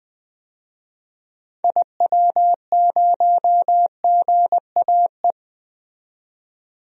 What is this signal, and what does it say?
Morse code, heard as a keyed tone, about 20 wpm, 700 Hz. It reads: IW0GAE
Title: DXCC-100 Call Signs - 6 Characters - 20wpm